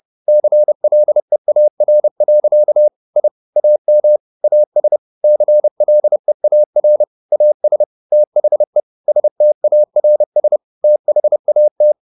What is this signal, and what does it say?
Morse code, heard as a keyed tone, about 30 wpm, 600 Hz. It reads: CLEAR. I AM AS CLEAR AS THE STARS THAT